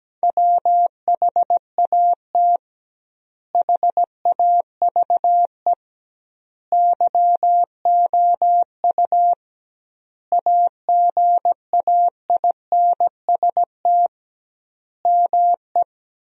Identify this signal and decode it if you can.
Morse code, heard as a keyed tone, about 17 wpm, 700 Hz. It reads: WHAT HAVE YOU AGAINST ME
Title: 2x Word-Spacing - Sentences from Top 400 Words - 17wpm